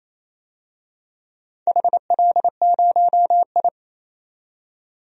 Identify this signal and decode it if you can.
Morse code, heard as a keyed tone, about 29 wpm, 700 Hz. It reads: HL0I
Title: DXCC-100 Call Signs - 4 Characters - 28wpm